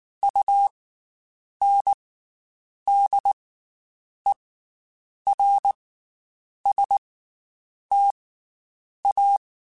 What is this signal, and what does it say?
Morse code, heard as a keyed tone, about 19 wpm, 800 Hz. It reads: U N D E R S T A
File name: Special INGs / Special INGs-mp3s / Slow ing @18